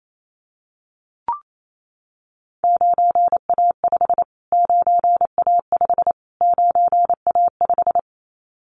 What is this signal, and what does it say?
Morse code, heard as a keyed tone, about 28 wpm, 700 Hz. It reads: 9A5 9A5 9A5
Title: Callsign Prefixes for Top 25 DXCC - Repeated 3x - 28wpm